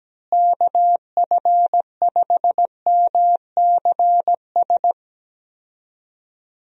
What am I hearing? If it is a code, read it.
Morse code, heard as a keyed tone, about 17 wpm, 700 Hz. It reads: KF5MCS
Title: USA Call Signs - 6 Characters - Repeated 2x - 17wpm